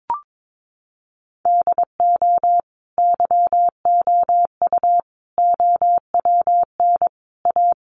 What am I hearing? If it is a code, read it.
Morse code, heard as a keyed tone, about 22 wpm, 700 Hz. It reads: DO YOU OWN A
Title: Sentences from Top 400 Words 22wpm